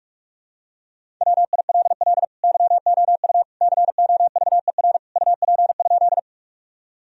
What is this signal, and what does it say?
Morse code, heard as a keyed tone, about 46 wpm, 700 Hz. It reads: WILL YOU COVER UP?
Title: Sentences from Top 1000 Words - Review Part 1 - 45wpm